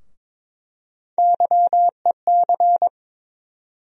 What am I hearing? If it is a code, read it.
Morse code, heard as a keyed tone, about 22 wpm, 700 Hz. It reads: YEC
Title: USA Call Sign Postfixes - 3 Letters 22wpm